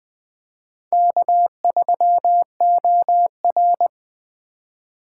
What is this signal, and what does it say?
Morse code, heard as a keyed tone, about 23 wpm, 700 Hz. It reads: K3OR
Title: Speed Racing - USA 1x2 Call Signs 20wpm